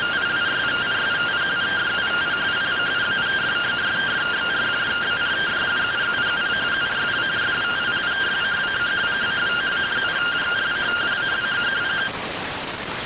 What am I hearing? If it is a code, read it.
Two-tone teleprinter signal, about 170 Hz shift.